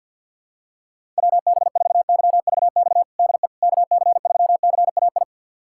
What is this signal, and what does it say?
Morse code, heard as a keyed tone, about 53 wpm, 700 Hz. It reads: WB4YVX DE KC3CAI
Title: USA Callsign DE Callsign - 6 Characters - 50wpm